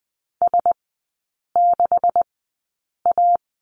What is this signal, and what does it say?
Morse code, heard as a keyed tone, about 20 wpm, spaced as for 8 wpm, 700 Hz. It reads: S6A
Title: 127 Lesson 21e - Number 6 - Three Characters 20wpm